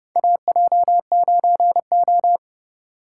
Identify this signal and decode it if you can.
Morse code, heard as a keyed tone, about 30 wpm, 700 Hz. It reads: AJ9O
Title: USA Call Signs - 4 Characters - 4x Character Spacing - 30wpm